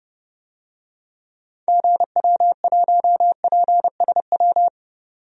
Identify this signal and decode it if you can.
Morse code, heard as a keyed tone, about 30 wpm, 700 Hz. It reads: GW1PSW
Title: DXCC-100 Call Signs - 6 Characters - 30wpm